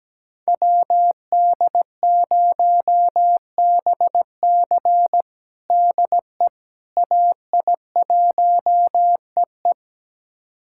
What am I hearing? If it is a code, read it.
Morse code, heard as a keyed tone, about 17 wpm, 700 Hz. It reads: WD0BC DE AI1EE
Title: USA Callsign DE Callsign - 5 Characters - 17wpm